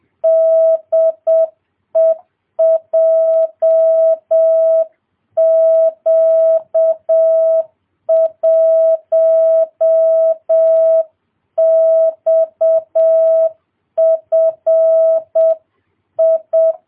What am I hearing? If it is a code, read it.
Morse code, heard as a keyed tone, about 7 wpm, 650 Hz. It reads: DE JQ1XFI